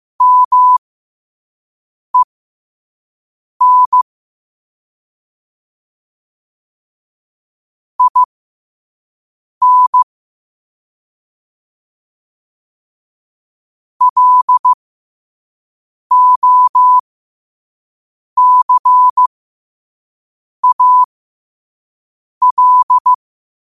Morse code, heard as a keyed tone, about 15 wpm, spaced as for 5 wpm, 1000 Hz. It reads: MEN IN LOCAL
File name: qotdfiles / 5wpm